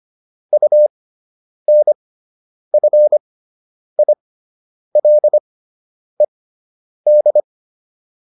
Morse code, heard as a keyed tone, about 25 wpm, spaced as for 9 wpm, 600 Hz. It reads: UNFILED